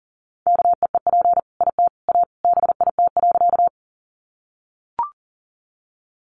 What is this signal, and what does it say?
Morse code, heard as a keyed tone, about 40 wpm, 700 Hz. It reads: KEEP IT A BIT.
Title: Sentences from Top 900 Words 40wpm